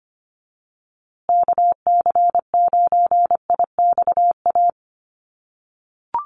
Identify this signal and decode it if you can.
Morse code, heard as a keyed tone, about 25 wpm, 700 Hz. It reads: KC9IXA
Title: USA Call Signs Repeated 3x 25wpm